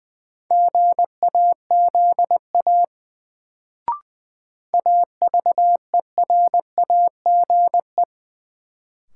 Morse code, heard as a keyed tone, about 21 wpm, 700 Hz. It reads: GAZA AVERAGE
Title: Sets of 5 Words - New Word Drill - 20wpm